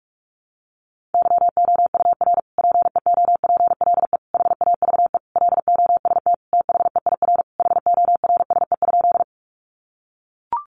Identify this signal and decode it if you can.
Morse code, heard as a keyed tone, about 44 wpm, 700 Hz. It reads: YOUR PEOPLE HAVE LOST THEIR HORSE?
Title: Sentences from Top 1400 Words 45wpm